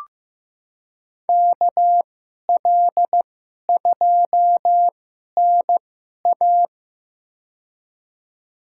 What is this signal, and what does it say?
Morse code, heard as a keyed tone, about 16 wpm, spaced as for 11 wpm, 700 Hz. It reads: KL2NA
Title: USA Call Signs - 2x Character Spacing - 15wpm